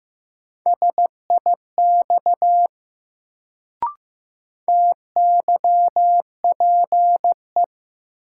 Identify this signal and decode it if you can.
Morse code, heard as a keyed tone, about 15 wpm, 700 Hz. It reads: SIX TYPE
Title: Sentences from Top 600 Words - New Word Drill - 15wpm